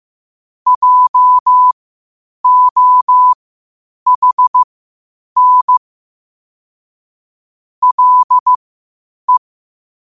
Morse code, heard as a keyed tone, about 15 wpm, spaced as for 8 wpm, 1000 Hz. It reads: JOHN LE
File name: qotdfiles / 8wpm